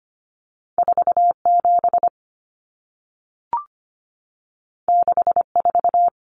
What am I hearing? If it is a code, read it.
Morse code, heard as a keyed tone, about 25 wpm, 700 Hz. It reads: 47 64
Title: Numbers - 2-Digit 25wpm